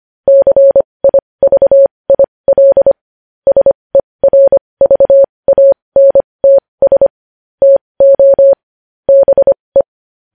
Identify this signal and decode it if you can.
Morse code, heard as a keyed tone, about 25 wpm, spaced as for 20 wpm, 550 Hz. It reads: CIVIL SERVANTS TO BE
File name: podcast / News headlines-1x-25x20